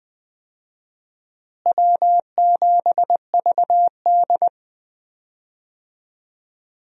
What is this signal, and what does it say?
Morse code, heard as a keyed tone, about 20 wpm, 700 Hz. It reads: W7VD
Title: USA Call Signs - 4 Characters - Repeated 2x - 20wpm